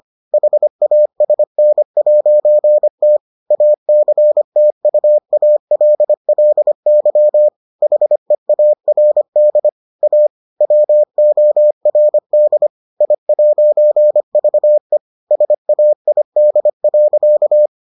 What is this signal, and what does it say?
Morse code, heard as a keyed tone, about 25 wpm, 600 Hz. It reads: HASN'T ACTUALLY HEARD A WORD I'VE SAID.